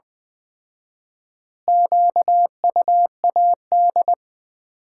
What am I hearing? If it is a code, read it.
Morse code, heard as a keyed tone, about 20 wpm, 700 Hz. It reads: QUAD
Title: Common QSO Phrases - 4x Word-Spacing - 20wpm